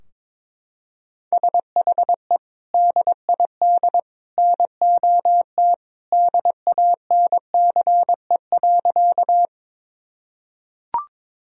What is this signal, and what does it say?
Morse code, heard as a keyed tone, about 22 wpm, 700 Hz. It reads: SHE DID NOT DANCE.
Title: Sentences from Top 1000 Words - Review Part 4 - 22wpm